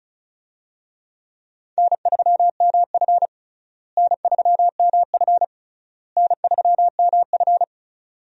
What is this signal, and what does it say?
Morse code, heard as a keyed tone, about 36 wpm, 700 Hz. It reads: N3MF N3MF N3MF
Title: USA Call Signs - 4 Characters - Repeated 3x - 35wpm